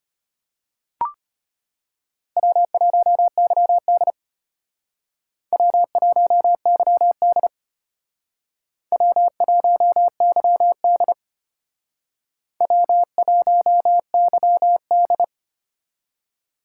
Morse code, heard as a keyed tone, about 33 wpm, 700 Hz. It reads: W1YD W1YD W1YD W1YD